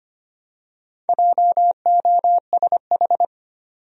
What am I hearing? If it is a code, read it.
Morse code, heard as a keyed tone, about 25 wpm, 700 Hz. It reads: JOSH
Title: USA Common Names - 25wpm